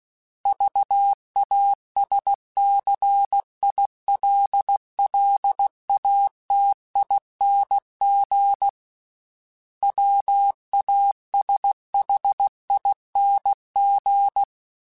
Morse code, heard as a keyed tone, about 16 wpm, 800 Hz. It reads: VASCILLATING WASHING
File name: Special INGs / Special INGs-mp3s / Sngl ing @15